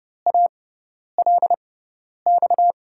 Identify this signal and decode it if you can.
Morse code, heard as a keyed tone, about 30 wpm, spaced as for 10 wpm, 700 Hz. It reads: ALX
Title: USA Call Signs - 5 Characters - 6x Character Spacing - 30wpm